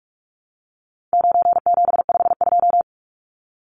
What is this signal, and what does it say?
Morse code, heard as a keyed tone, about 44 wpm, 700 Hz. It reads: 9752
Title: Numbers - 4-Digit 45wpm